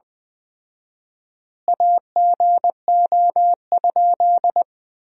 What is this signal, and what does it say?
Morse code, heard as a keyed tone, about 20 wpm, 700 Hz. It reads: AGO?
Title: 4x Word-Spacing - Sentences from Top 500 Words - 20wpm